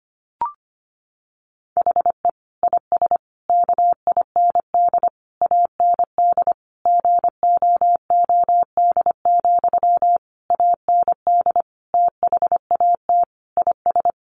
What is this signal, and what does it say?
Morse code, heard as a keyed tone, about 25 wpm, 700 Hz. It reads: HE IS KIND AND GOOD, AND THAT IS
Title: Sentences from Top 1400 Words 25wpm